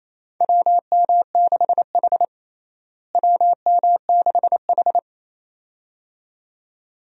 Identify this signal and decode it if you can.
Morse code, heard as a keyed tone, about 28 wpm, 700 Hz. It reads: WM6H WM6H
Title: USA Call Signs - 4 Characters - Repeated 2x - 28wpm